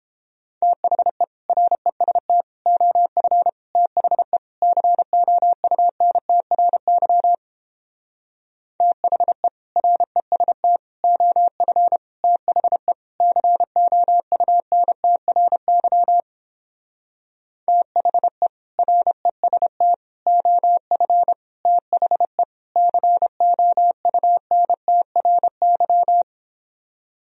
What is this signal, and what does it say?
Morse code, heard as a keyed tone, about 29 wpm, 700 Hz. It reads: THE REST OF THE COUNTRY THE REST OF THE COUNTRY THE REST OF THE COUNTRY